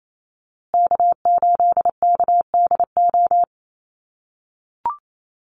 Morse code, heard as a keyed tone, about 28 wpm, 700 Hz. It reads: K8KDO
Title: USA Call Signs Repeated 2x 28wpm